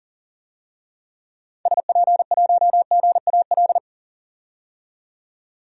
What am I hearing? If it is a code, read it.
Morse code, heard as a keyed tone, about 40 wpm, 700 Hz. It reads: SP1GAL
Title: DXCC-100 Call Signs - 6 Characters - 40wpm